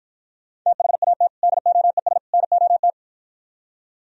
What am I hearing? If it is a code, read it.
Morse code, heard as a keyed tone, about 52 wpm, 700 Hz. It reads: THAT DOES NOT